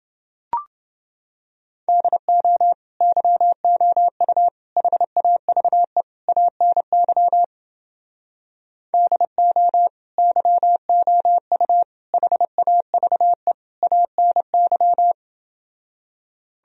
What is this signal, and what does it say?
Morse code, heard as a keyed tone, about 30 wpm, 700 Hz. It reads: DO YOU HAVE ANY DO YOU HAVE ANY